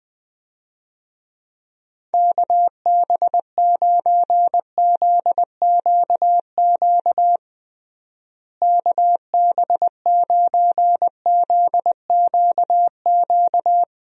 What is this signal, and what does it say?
Morse code, heard as a keyed tone, about 21 wpm, 700 Hz. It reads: KB9ZQQ KB9ZQQ
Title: USA Call Signs - 6 Characters - Repeated 2x - 20wpm